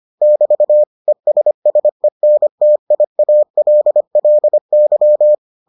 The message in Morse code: <BT> ESSENTIALLY

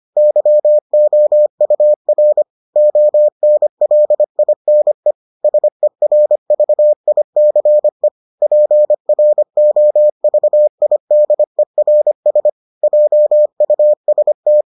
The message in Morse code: YOUR ONLINE SERVICE PROVIDERS JUST

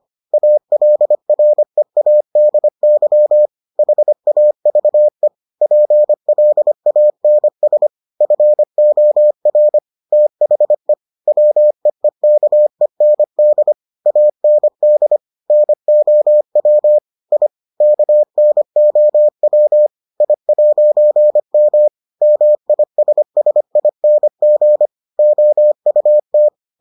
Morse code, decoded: ALREADY HAVE PLANS FOR THE WEEKEND AND NOW I KNOW I'M MISSING OUT

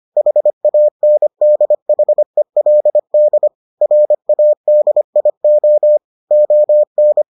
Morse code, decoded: HANDHELD RADIO ON

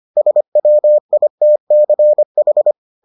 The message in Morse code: SWITCH

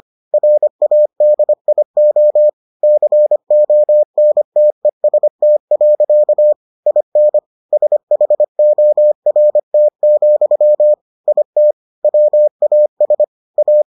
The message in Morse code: RADIO CONTEST. IN SHORT, IT WAS A

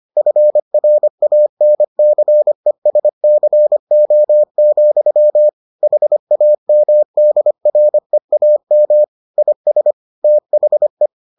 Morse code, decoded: FRANCESCO, HAMDREAM IS THE

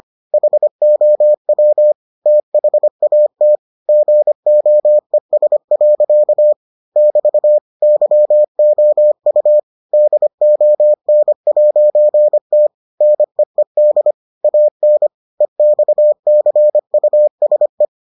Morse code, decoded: HOW THAT GOES. <BT> YOU DON'T NEED AN EXCUSE

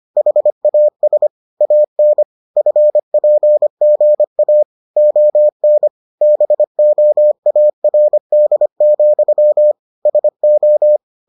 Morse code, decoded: HAS AN FPGA ON BOARD, SO

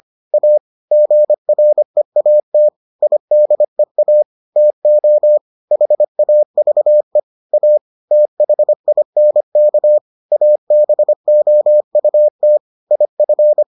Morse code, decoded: A GREAT IDEA TO HAVE A THINK ABOUT IF

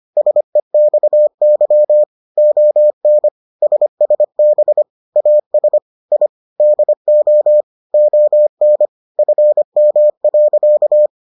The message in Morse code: SEXY ON SSB AS I DO ON FM.

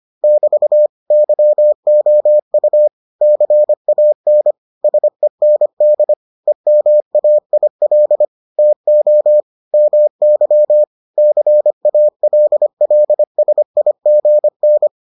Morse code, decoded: <BT> YOU CAN SEND EMAIL TO MY CALLSIGN

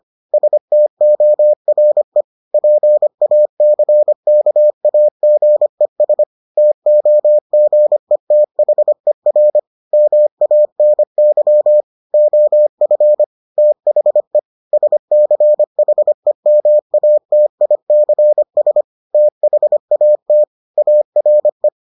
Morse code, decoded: STORE PACKAGES TOGETHER MANY OF THE SCHEMATICS THAT ARE

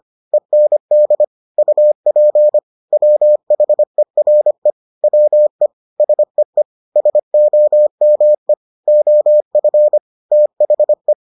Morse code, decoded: END UP WHERE WE SEE SOME OF THE

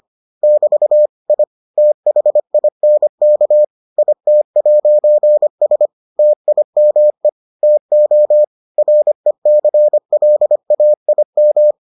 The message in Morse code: <BT> I THINK IT'S TIME TO RECLAIM

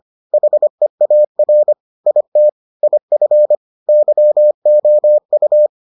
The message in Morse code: HEAR IT IF YOU

